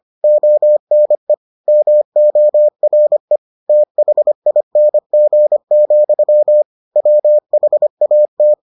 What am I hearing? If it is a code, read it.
Morse code, heard as a keyed tone, about 25 wpm, 600 Hz. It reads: ONE MORE THING, WHAT